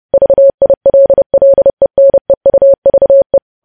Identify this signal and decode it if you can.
Morse code, heard as a keyed tone, about 30 wpm, 550 Hz. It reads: VILLENEUVE
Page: News Headlines in Morse Code at 30 WPM